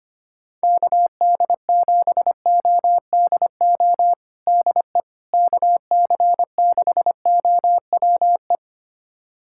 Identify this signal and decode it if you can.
Morse code, heard as a keyed tone, about 25 wpm, 700 Hz. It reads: KD7ODO DE KC6OWE
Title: USA Callsign DE Callsign - 6 Characters - 25wpm